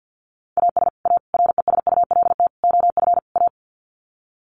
Morse code, heard as a keyed tone, about 50 wpm, 700 Hz. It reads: AS A RESULT OF A
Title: Sets of 5 Words Encore 50wpm